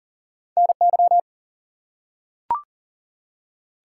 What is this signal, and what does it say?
Morse code, heard as a keyed tone, about 43 wpm, 700 Hz. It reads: NY
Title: US States and Canadian Provinces 40wpm